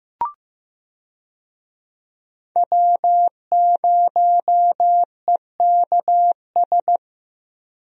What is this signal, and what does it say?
Morse code, heard as a keyed tone, about 15 wpm, 700 Hz. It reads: W0EKS
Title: USA Call Signs - 5 Characters - Repeated 3x - 15wpm